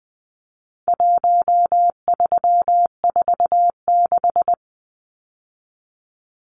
Morse code, heard as a keyed tone, about 20 wpm, 700 Hz. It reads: 1346
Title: Numbers - Contest Style - 4-Digit 20wpm